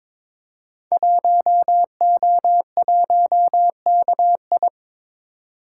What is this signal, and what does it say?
Morse code, heard as a keyed tone, about 22 wpm, 690 Hz. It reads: 1O1KI
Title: Code Groups - 5 Letter:Numbers - 22wpm